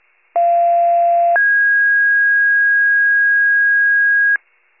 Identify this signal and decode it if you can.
Two-tone page, heard as a one-second tone, then a three-second tone, about 690 then 1650 Hz. It is heard from 0.4 to 4.4 s.